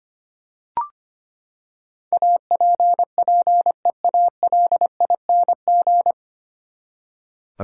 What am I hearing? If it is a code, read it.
Morse code, heard as a keyed tone, about 25 wpm, 700 Hz. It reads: APPEALING
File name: Nine Letter Words - 25wpm